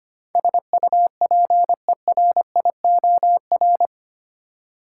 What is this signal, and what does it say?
Morse code, heard as a keyed tone, about 25 wpm, 700 Hz. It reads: SUPERIOR
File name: Eight Letter Words - 25wpm